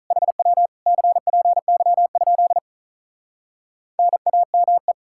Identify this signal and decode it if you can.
Morse code, heard as a keyed tone, about 46 wpm, 700 Hz. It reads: HW CPY? NAME